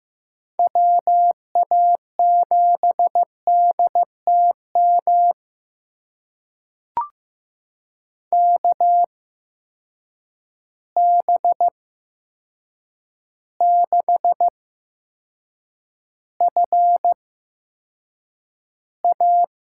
Morse code, heard as a keyed tone, about 15 wpm, spaced as for 7 wpm, 700 Hz. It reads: WA7DTM KB6FA
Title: USA Call Signs - 6 Characters - 8x Character Spacing - 15wpm